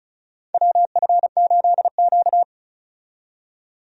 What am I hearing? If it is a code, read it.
Morse code, heard as a keyed tone, about 34 wpm, 700 Hz. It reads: WF8Q
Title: USA Call Signs - 4 Characters - 35wpm